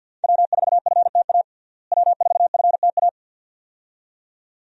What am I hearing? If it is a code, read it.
Morse code, heard as a keyed tone, about 50 wpm, 700 Hz. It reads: W4FTA W4FTA
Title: USA Call Signs - 5 Characters - Repeated 2x - 50wpm